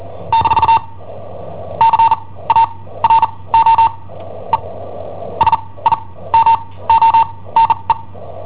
Morse code, heard as a keyed tone, about 39 wpm, spaced as for 20 wpm, 910 Hz. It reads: - CARO E SIMONE
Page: Cassetta esercizi preparata da I7OHP per IW5ELP